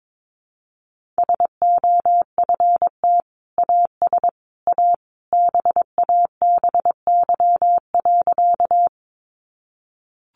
Morse code, heard as a keyed tone, about 22 wpm, 700 Hz. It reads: SOFT AS A BABY.
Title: Sentences from Top 1500 Words 22wpm